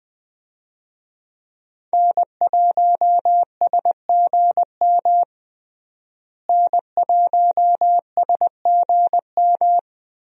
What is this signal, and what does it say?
Morse code, heard as a keyed tone, about 21 wpm, 700 Hz. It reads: N1SGM N1SGM
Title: USA Call Signs - 5 Characters - Repeated 2x - 20wpm